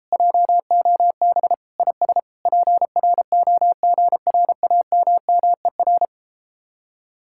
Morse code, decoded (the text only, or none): JOB IS PROGRAMMER